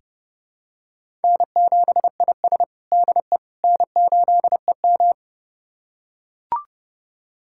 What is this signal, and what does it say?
Morse code, heard as a keyed tone, about 30 wpm, 700 Hz. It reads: N7IS DE N8EM
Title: USA Callsign DE Callsign - 4 Characters - 30wpm